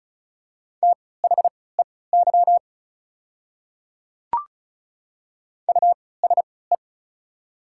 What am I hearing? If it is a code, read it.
Morse code, heard as a keyed tone, about 34 wpm, spaced as for 20 wpm, 690 Hz. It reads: THEY USE
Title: 3x Character Spacing - Top 100 Words - 35wpm